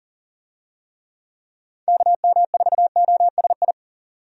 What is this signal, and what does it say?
Morse code, heard as a keyed tone, about 40 wpm, 700 Hz. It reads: KM4OSI
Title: USA Call Signs - 6 Characters - 40wpm